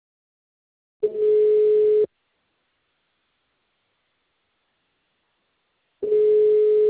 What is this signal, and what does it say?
Ringback tone (Europe, 425 Hz).